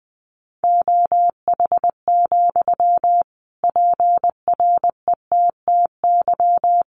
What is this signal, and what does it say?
Morse code, heard as a keyed tone, about 20 wpm, 700 Hz. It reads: OH, PRETTY
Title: Sentences from Top 1300 Words 20wpm